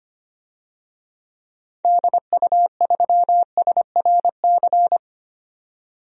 Morse code, heard as a keyed tone, about 25 wpm, 700 Hz. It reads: DU3SRC
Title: DXCC-100 Call Signs - 6 Characters - 25wpm